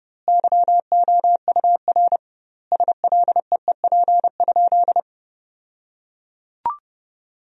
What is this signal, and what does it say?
Morse code, heard as a keyed tone, about 32 wpm, 700 Hz. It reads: YOUR SLEEP?
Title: 2x Word-Spacing - Sentences from Top 500 Words - 30wpm